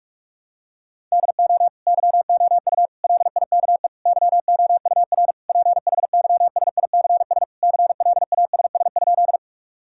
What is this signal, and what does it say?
Morse code, heard as a keyed tone, about 46 wpm, 700 Hz. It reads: DO YOU LIKE YOUR PHYSICS CLASS?
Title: Sentences from Top 1000 Words - Review Part 5 - 45wpm